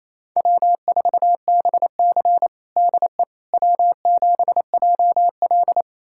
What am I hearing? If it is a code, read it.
Morse code, heard as a keyed tone, about 28 wpm, 700 Hz. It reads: W4BC DE W7JL
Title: USA Callsign DE Callsign - 4 Characters - 28wpm